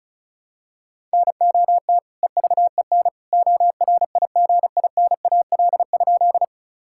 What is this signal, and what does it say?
Morse code, heard as a keyed tone, about 36 wpm, 700 Hz. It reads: NOT EVEN ORIGINAL?
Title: Sentences from Top 1000 Words - Review Part 6 - 35wpm